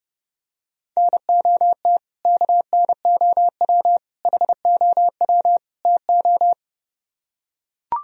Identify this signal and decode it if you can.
Morse code, heard as a keyed tone, about 30 wpm, 700 Hz. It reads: NOT KNOW HOW TO